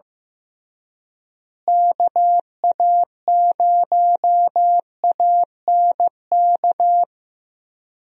Morse code, heard as a keyed tone, about 15 wpm, 700 Hz. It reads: KA0ANK
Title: USA Call Signs - 6 Characters - Repeated 3x - 15wpm